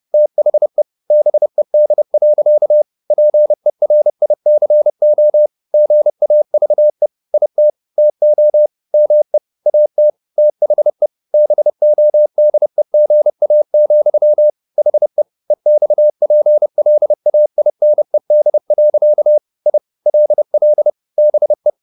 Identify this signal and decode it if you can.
Morse code, heard as a keyed tone, about 30 wpm, 600 Hz. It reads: THE BED. PERICO GAVE IT TO ME AT THE BODEGA, HE EXPLAINED. I LL BE
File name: the old man and the sea / the old man and the sea 1 30wpm